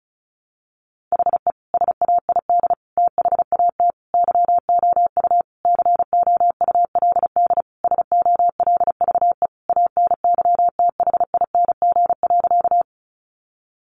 Morse code, heard as a keyed tone, about 34 wpm, 700 Hz. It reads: HE SAID THAT YOU COULD SOLVE ANYTHING.
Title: Sentences from Top 1000 Words 35wpm